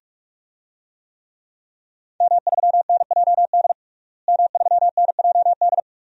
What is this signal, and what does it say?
Morse code, heard as a keyed tone, about 46 wpm, 700 Hz. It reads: M3NJD M3NJD
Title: DXCC-100 Call Signs - 5 Characters - Repeated 2x - 45wpm